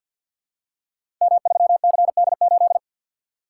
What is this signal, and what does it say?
Morse code, heard as a keyed tone, about 46 wpm, 700 Hz. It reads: M3CD8
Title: Code Groups - 5 Letter:Numbers - 50wpm